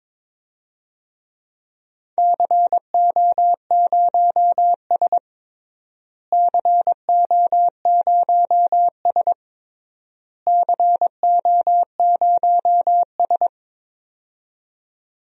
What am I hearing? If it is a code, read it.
Morse code, heard as a keyed tone, about 22 wpm, 700 Hz. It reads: CO0S CO0S CO0S
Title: DXCC-100 Call Signs - 4 Characters - Repeated 3x - 22wpm